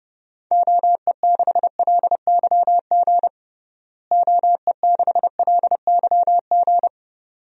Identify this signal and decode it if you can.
Morse code, heard as a keyed tone, about 32 wpm, 700 Hz. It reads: OE6LYG OE6LYG
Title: DXCC-100 Call Signs - 6 Characters - Repeated 2x - 30wpm